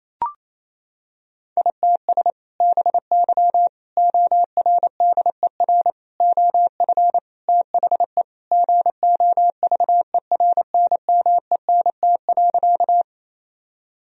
Morse code, heard as a keyed tone, about 27 wpm, 700 Hz. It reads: ITS BY ORDER OF THE GOVERNMENT.
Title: Sentences from Top 1000 Words - Review Part 3 - 28wpm